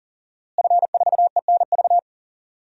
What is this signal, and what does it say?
Morse code, heard as a keyed tone, about 40 wpm, 700 Hz. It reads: F4ENV
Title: DXCC-100 Call Signs - 5 Characters - Repeated 2x - 40wpm